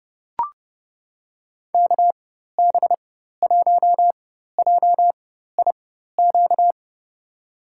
Morse code, heard as a keyed tone, about 32 wpm, spaced as for 14 wpm, 700 Hz. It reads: KB1JIQ
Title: USA Call Signs - 4x Character Spacing - 30wpm